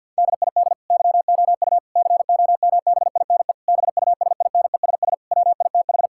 Morse code, heard as a keyed tone, about 50 wpm, 700 Hz. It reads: DID YOU COMBINE BUSINESS WITH
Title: Sentences from Top 1600 Words 50wpm